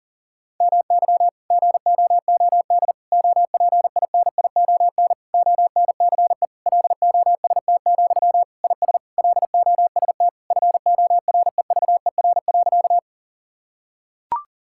Morse code, heard as a keyed tone, about 40 wpm, 700 Hz. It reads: MY GOOD OPINION ONCE LOST, IS LOST FOREVER.
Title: Sentences from Top 1600 Words 40wpm